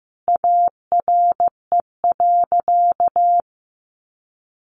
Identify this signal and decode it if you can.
Morse code, heard as a keyed tone, about 15 wpm, 700 Hz. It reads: ARE.
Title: Sentences from Top 1200 Words 15wpm